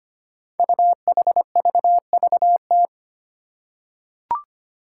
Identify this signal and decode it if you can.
Morse code, heard as a keyed tone, about 25 wpm, 700 Hz. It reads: UHVVT
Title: Five Challenging Letter-Number Combinations 25wpm